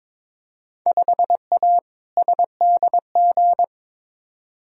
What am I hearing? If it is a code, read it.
Morse code, heard as a keyed tone, about 22 wpm, 700 Hz. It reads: 5A SDG
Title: Field Day Exchanges Encore - 22wpm